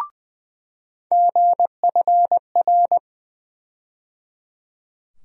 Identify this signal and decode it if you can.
Morse code, heard as a keyed tone, about 20 wpm, 700 Hz. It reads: GFR
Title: USA Call Sign Postfixes - 3 Letters 20wpm